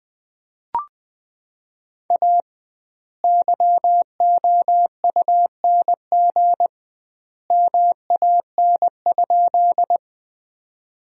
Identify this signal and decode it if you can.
Morse code, heard as a keyed tone, about 21 wpm, 700 Hz. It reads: A YOUNG MAN?
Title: 2x Word-Spacing - Sentences from Top 400 Words - 20wpm